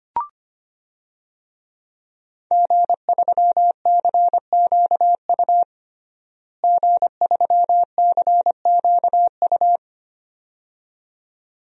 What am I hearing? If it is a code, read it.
Morse code, heard as a keyed tone, about 25 wpm, 700 Hz. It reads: G3CQU G3CQU
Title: DXCC-100 Call Signs - 5 Characters - Repeated 2x - 25wpm